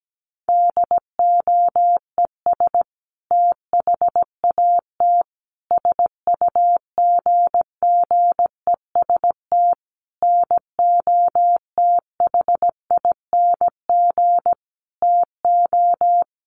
Morse code, heard as a keyed tone, about 17 wpm, 700 Hz. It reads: DOES THAT SUGGEST NOTHING TO
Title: Sentences from Top 1300 Words 17wpm